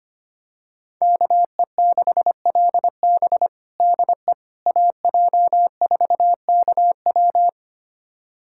Morse code, heard as a keyed tone, about 25 wpm, 700 Hz. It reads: KE6LB DE AJ4KW
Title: USA Callsign DE Callsign - 5 Characters - 25wpm